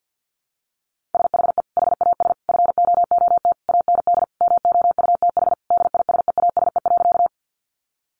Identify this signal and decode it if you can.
Morse code, heard as a keyed tone, about 50 wpm, 700 Hz. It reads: SHE HAS FOOT AND MOUTH DISEASE.
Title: Sentences from Top 1500 Words 50wpm